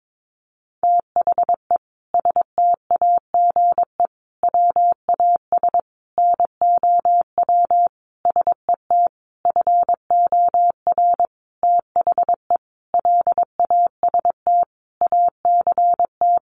Morse code, heard as a keyed tone, about 22 wpm, 700 Hz. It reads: THE STAGE WAS NOW SET FOR THE LAST ACT
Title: Sentences from Top 1500 Words 22wpm